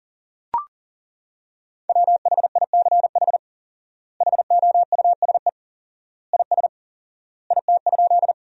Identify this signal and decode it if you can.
Morse code, heard as a keyed tone, about 40 wpm, 700 Hz. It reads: WHICH HOUSE IS IT?
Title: 4x Word-Spacing - Sentences from Top 200 Words - 40wpm